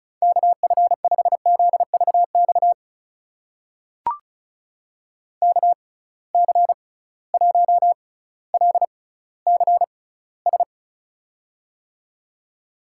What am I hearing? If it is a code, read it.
Morse code, heard as a keyed tone, about 36 wpm, spaced as for 18 wpm, 700 Hz. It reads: KF5ZVX KC1LCS
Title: USA Call Signs - 6 Characters - 6x Character Spacing - 35wpm